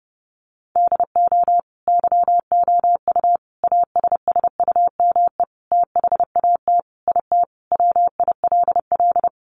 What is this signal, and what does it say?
Morse code, heard as a keyed tone, about 28 wpm, 700 Hz. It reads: DO YOU ASSUME THAT IT WILL
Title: Sentences from Top 1500 Words 30wpm